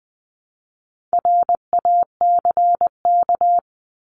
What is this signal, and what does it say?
Morse code, heard as a keyed tone, about 20 wpm, 700 Hz. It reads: RACK
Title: 260 Lesson 42e - Words 20-20wpm